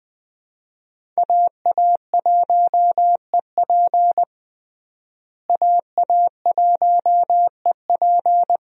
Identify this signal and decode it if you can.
Morse code, heard as a keyed tone, about 20 wpm, 700 Hz. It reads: AA1EP AA1EP
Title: USA Call Signs - 5 Characters - Repeated 2x - 20wpm